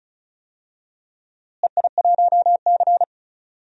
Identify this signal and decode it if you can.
Morse code, heard as a keyed tone, about 34 wpm, 700 Hz. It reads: EI1C
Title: DXCC-100 Call Signs - 4 Characters - Repeated 3x - 35wpm